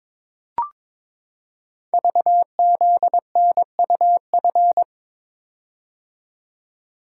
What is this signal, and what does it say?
Morse code, heard as a keyed tone, about 23 wpm, 700 Hz. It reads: VZNUF
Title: Code Groups - 5 Letter:Numbers - 22wpm